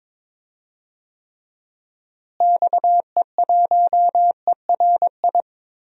Morse code, heard as a keyed tone, about 22 wpm, 700 Hz. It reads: XE1ERI
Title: DXCC-100 Call Signs - 6 Characters - Repeated 3x - 22wpm